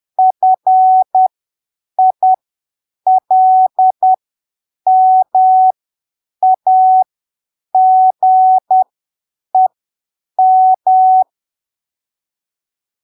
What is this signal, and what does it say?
Morse code, heard as a keyed tone, about 10 wpm, spaced as for 7 wpm, 750 Hz. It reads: FILMAGEM
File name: arquivos / texto / texto 09 10ppm